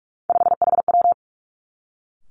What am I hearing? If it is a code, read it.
Morse code, heard as a keyed tone, about 46 wpm, 700 Hz. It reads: 5HW
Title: Three Challenging Letter-Number Combinations 45wpm